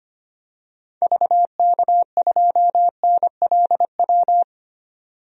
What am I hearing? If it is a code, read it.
Morse code, heard as a keyed tone, about 25 wpm, 700 Hz. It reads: VK2NLW
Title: DXCC-100 Call Signs - 6 Characters - 25wpm